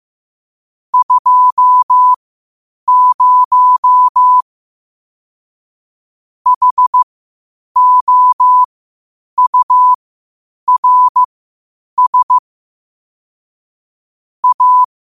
Morse code, heard as a keyed tone, about 15 wpm, spaced as for 8 wpm, 1000 Hz. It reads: 20 HOURS A
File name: qotdfiles / 8wpm